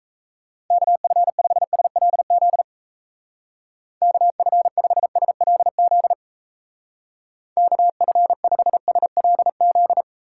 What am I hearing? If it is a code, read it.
Morse code, heard as a keyed tone, about 35 wpm, 700 Hz. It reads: KF5SLZ KF5SLZ KF5SLZ